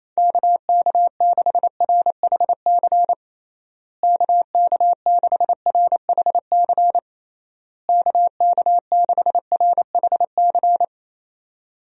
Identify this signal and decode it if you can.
Morse code, heard as a keyed tone, about 28 wpm, 700 Hz. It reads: KK6RHC KK6RHC KK6RHC
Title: USA Call Signs - 6 Characters - Repeated 3x - 28wpm